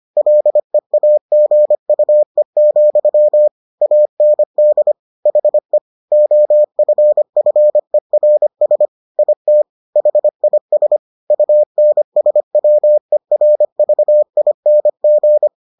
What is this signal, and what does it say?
Morse code, heard as a keyed tone, about 25 wpm, 600 Hz. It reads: LEAGUE, AND HE OFFERS IT HIS UNSWERVING